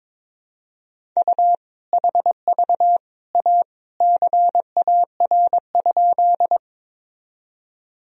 Morse code, heard as a keyed tone, about 22 wpm, 700 Hz. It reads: U HV A CAR?
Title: Sentences from Top 300 Words and Abbreviations 22wpm